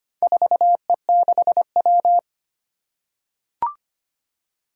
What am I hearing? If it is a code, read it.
Morse code, heard as a keyed tone, about 25 wpm, 700 Hz. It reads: 4E6W
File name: practice / 25wpm / Four Letter-Number Combinations - 25wpm